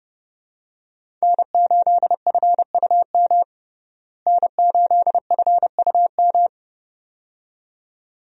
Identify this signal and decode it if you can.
Morse code, heard as a keyed tone, about 30 wpm, 700 Hz. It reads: N8FUM N8FUM
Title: USA Call Signs - 5 Characters - Repeated 2x - 30wpm